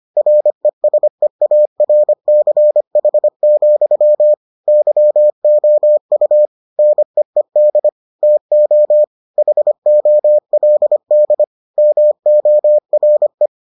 Morse code, heard as a keyed tone, about 25 wpm, 600 Hz. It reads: RESEARCH, YOU NEED TO HOLD MORE